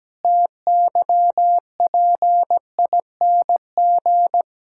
Morse code, heard as a keyed tone, about 17 wpm, 700 Hz. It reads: TYPING
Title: Pangram Challenge v2 - 17wpm